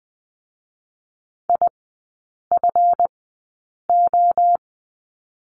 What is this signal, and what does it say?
Morse code, heard as a keyed tone, about 20 wpm, spaced as for 8 wpm, 690 Hz. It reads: IFO
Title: 205 Lesson 33e - Letter j - Three Characters 20wpm